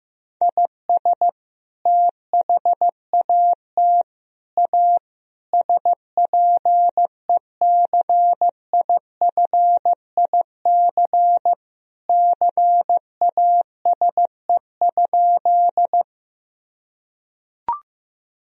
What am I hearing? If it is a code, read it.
Morse code, heard as a keyed tone, about 15 wpm, 700 Hz. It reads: IS THAT A SPECIFIC CASE?
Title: Sentences from Top 1000 Words - Review Part 3 - 15wpm